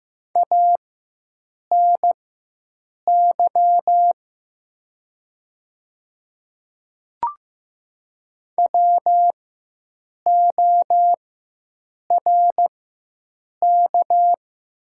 Morse code, heard as a keyed tone, about 15 wpm, spaced as for 7 wpm, 700 Hz. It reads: ANY WORK
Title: 4x Character Spacing - Top 300 Words - 15wpm